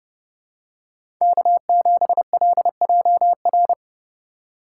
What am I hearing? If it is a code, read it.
Morse code, heard as a keyed tone, about 30 wpm, 700 Hz. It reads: K7LJR